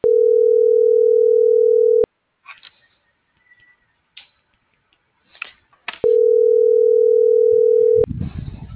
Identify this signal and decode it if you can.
Ringback tone (North America), 440 + 480 Hz.